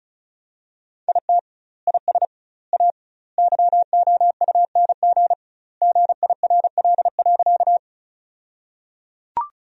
Morse code, heard as a keyed tone, about 36 wpm, 700 Hz. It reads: IT IS A YOUNG GIRL.
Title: 2x Word-Spacing - Sentences from Top 400 Words - 35wpm